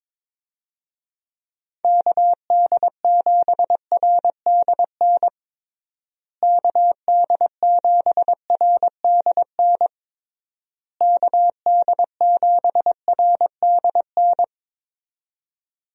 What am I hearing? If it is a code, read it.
Morse code, heard as a keyed tone, about 21 wpm, 700 Hz. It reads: KD7RDN KD7RDN KD7RDN
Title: USA Call Signs - 6 Characters - Repeated 3x - 22wpm